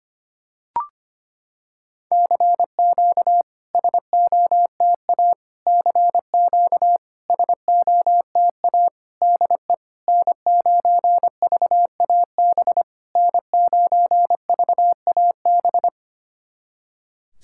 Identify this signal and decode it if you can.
Morse code, heard as a keyed tone, about 25 wpm, 700 Hz. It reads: CQ SOTA CQ SOTA DE N9VAB N9VAB
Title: SOTA QSO Practice 25wpm